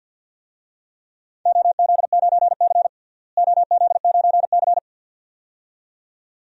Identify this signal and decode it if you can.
Morse code, heard as a keyed tone, about 49 wpm, 700 Hz. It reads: OZ9C OZ9C
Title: DXCC-100 Call Signs - 4 Characters - Repeated 2x - 50wpm